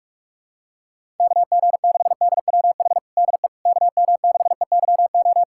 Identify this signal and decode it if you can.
Morse code, heard as a keyed tone, about 46 wpm, 700 Hz. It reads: KG6DWH DE KM6EYO
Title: USA Callsign DE Callsign - 6 Characters - 45wpm